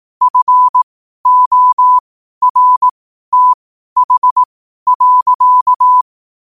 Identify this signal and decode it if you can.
Morse code, heard as a keyed tone, about 18 wpm, spaced as for 12 wpm, 1000 Hz. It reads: FORTH.
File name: qotdfiles / 12wpm